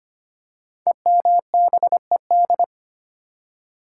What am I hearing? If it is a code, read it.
Morse code, heard as a keyed tone, about 25 wpm, 700 Hz. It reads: EMBED
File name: 5 Letter Words - First 3 Letters Given - 25wpm